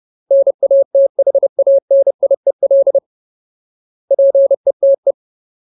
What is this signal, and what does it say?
Morse code, heard as a keyed tone, about 30 wpm, 550 Hz. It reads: NATHANIEL PETE
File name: Ultimate Word List / 30WPM Ultimate